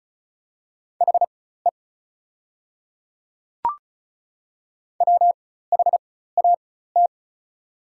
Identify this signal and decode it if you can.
Morse code, heard as a keyed tone, about 36 wpm, spaced as for 16 wpm, 700 Hz. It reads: HE WHAT
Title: 4x Character Spacing - Top 200 Words - 35wpm